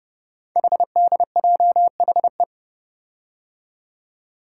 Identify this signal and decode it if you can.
Morse code, heard as a keyed tone, about 28 wpm, 700 Hz. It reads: HDJHE
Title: Code Groups - 5 Letters - 30wpm